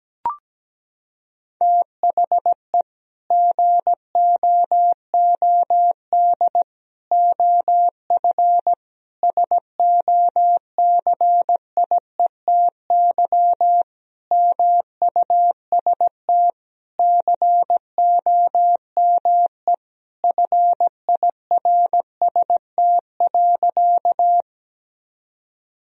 Morse code, heard as a keyed tone, about 17 wpm, 700 Hz. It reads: THE GOOD OF SOCIETY MUST COME FIRST.
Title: Sentences from Top 1000 Words - Review Part 4 - 17wpm